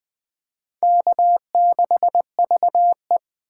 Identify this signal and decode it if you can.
Morse code, heard as a keyed tone, about 20 wpm, 700 Hz. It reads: K6VE
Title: Speed Racing - USA Call Signs 17wpm